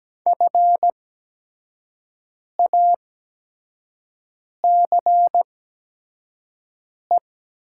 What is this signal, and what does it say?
Morse code, heard as a keyed tone, about 17 wpm, spaced as for 5 wpm, 700 Hz. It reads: FACE
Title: 8x Character Spacing - Top 500 Words - 17wpm